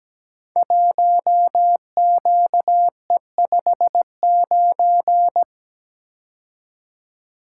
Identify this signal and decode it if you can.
Morse code, heard as a keyed tone, about 17 wpm, 700 Hz. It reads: 1QE59
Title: Code Groups - 5 Letter:Numbers - 17wpm